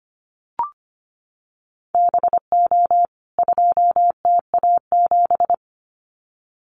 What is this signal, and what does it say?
Morse code, heard as a keyed tone, about 25 wpm, 700 Hz. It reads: BO 2TA7
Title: CWT Exchange 25wpm